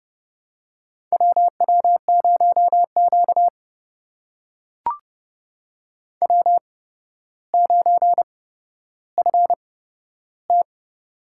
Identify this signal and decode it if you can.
Morse code, heard as a keyed tone, about 30 wpm, spaced as for 13 wpm, 700 Hz. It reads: WW0Q W9FT
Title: USA Call Signs - 8x Character Spacing - 30wpm